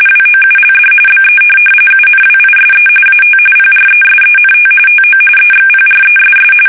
RTTY, about 45.45 baud. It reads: WOULD BE UNWISE AT PRESENT TO EXPRESS